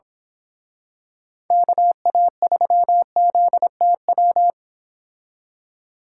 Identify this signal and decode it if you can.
Morse code, heard as a keyed tone, about 26 wpm, 700 Hz. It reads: KA3ZTW
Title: Speed Racing - USA 2x3 Call Signs 22wpm